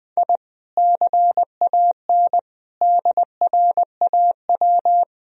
Morse code, heard as a keyed tone, about 20 wpm, 700 Hz. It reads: I CAN DRAW
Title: Sentences from Top 1000 Words - Review Part 1 - 20wpm